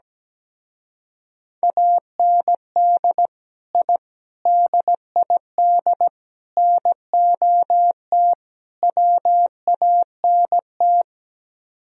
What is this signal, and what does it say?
Morse code, heard as a keyed tone, about 17 wpm, 700 Hz. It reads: AND I DID NOT WANT
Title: Speed-Racing - Sets of 5 Words 17wpm